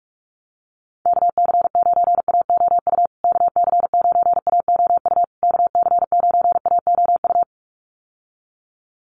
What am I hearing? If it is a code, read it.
Morse code, heard as a keyed tone, about 44 wpm, 700 Hz. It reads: KC9AOU KC9AOU KC9AOU
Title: USA Call Signs Repeated 3x 45wpm